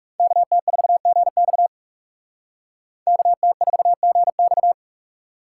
Morse code, heard as a keyed tone, about 44 wpm, 700 Hz. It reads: KT4GX KT4GX